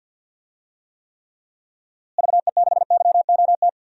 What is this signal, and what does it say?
Morse code, heard as a keyed tone, about 48 wpm, 700 Hz. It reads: VE6YOT
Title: DXCC-100 Call Signs - 6 Characters - Repeated 3x - 50wpm